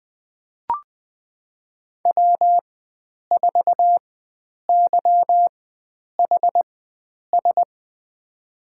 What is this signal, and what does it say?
Morse code, heard as a keyed tone, about 20 wpm, spaced as for 9 wpm, 700 Hz. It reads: W4YHS
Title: USA Call Signs - 4x Character Spacing - 20wpm